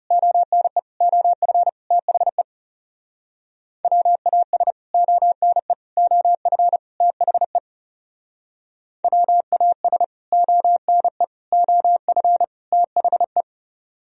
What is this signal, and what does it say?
Morse code, heard as a keyed tone, about 40 wpm, 700 Hz. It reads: ONE OF THE WAS ONE OF THE WAS ONE OF THE